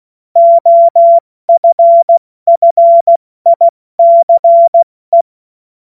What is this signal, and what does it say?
Morse code, heard as a keyed tone, about 16 wpm, 680 Hz. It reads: OFFICE